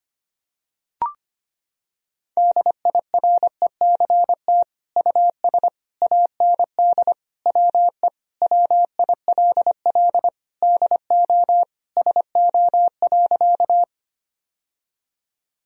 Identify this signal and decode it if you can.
Morse code, heard as a keyed tone, about 25 wpm, 700 Hz. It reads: DIRECT US AND WE WILL DO SO.
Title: Sentences from Top 1000 Words - Review Part 2 - 25wpm